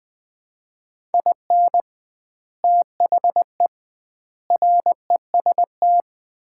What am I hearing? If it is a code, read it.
Morse code, heard as a keyed tone, about 20 wpm, 700 Hz. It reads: IN THE REST
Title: 2x Word-Spacing - Sets of 5 Words - 20wpm